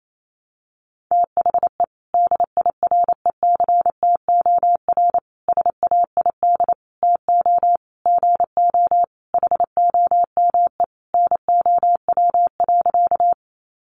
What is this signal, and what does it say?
Morse code, heard as a keyed tone, about 28 wpm, 700 Hz. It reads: THE DIRECTOR SAID TO GO HOME NOW.
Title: Sentences from Top 1300 Words 28wpm